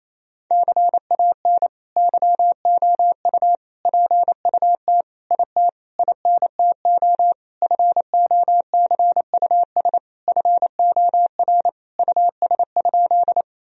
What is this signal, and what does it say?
Morse code, heard as a keyed tone, about 28 wpm, 700 Hz. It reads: CAN YOU PUT IT INTO FOCUS FOR US?
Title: Sentences from Top 1000 Words - Review Part 4 - 28wpm